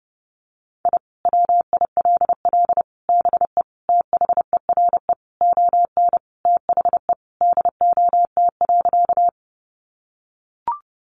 Morse code, heard as a keyed tone, about 30 wpm, 700 Hz. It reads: I WILL BE THERE ON THE DOT.
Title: Sentences from Top 1500 Words 30wpm